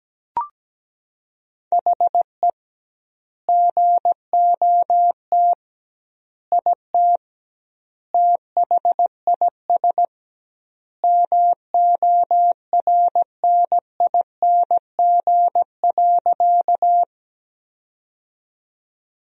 Morse code, heard as a keyed tone, about 17 wpm, 700 Hz. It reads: HE GOT IT THIS MORNING.
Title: 2x Word-Spacing - Sentences from Top 400 Words - 17wpm